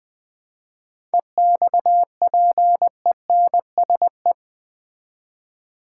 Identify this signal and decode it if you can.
Morse code, heard as a keyed tone, about 20 wpm, 700 Hz. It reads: EXPENSE
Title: 7 Letter Words - First 3 Letters Given - 20wpm